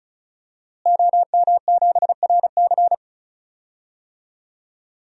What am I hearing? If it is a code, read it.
Morse code, heard as a keyed tone, about 36 wpm, 700 Hz. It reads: OM7RC
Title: DXCC-100 Call Signs - 5 Characters - Repeated 2x - 35wpm